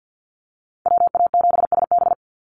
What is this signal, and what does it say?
Morse code, heard as a keyed tone, about 48 wpm, 700 Hz. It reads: WA7SB
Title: NA SOTA Chaser Callsigns 50wpm